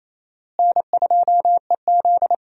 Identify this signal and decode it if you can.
Morse code, heard as a keyed tone, about 28 wpm, 700 Hz. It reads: N2EZ
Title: USA Call Signs - 4 Characters - Repeated 3x - 28wpm